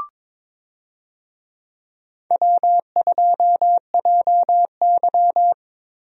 Morse code, heard as a keyed tone, about 22 wpm, 700 Hz. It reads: W2JY
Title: USA Call Signs - 4 Characters - Repeated 3x - 22wpm